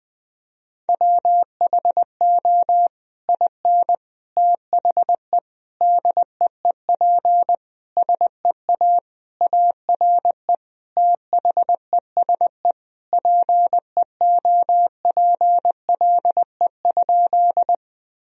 Morse code, heard as a keyed tone, about 20 wpm, 700 Hz. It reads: WHO IN THE DEEP SEA ARE THESE PEOPLE?
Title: Sentences from Top 1000 Words - Review Part 3 - 20wpm